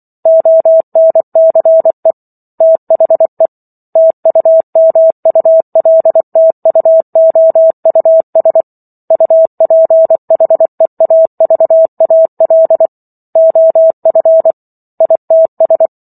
Morse code, heard as a keyed tone, about 24 wpm, 650 Hz. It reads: ONCE THE TUMULTUOUS UPHEAVAL OF ITS